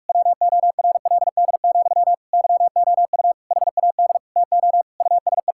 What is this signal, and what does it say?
Morse code, heard as a keyed tone, about 41 wpm, 700 Hz. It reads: WORLD, YOU HAD TO USE
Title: Sentences from Top 1000 Words - Review Part 6 - 45wpm